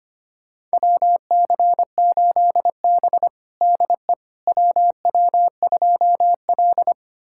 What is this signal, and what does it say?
Morse code, heard as a keyed tone, about 25 wpm, 700 Hz. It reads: WC8B DE WW2L
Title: USA Callsign DE Callsign - 4 Characters - 25wpm